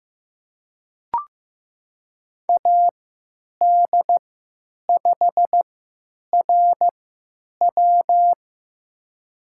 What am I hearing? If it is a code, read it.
Morse code, heard as a keyed tone, about 15 wpm, spaced as for 9 wpm, 700 Hz. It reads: AD5RW
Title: USA Call Signs - 3x Character Spacing - 15wpm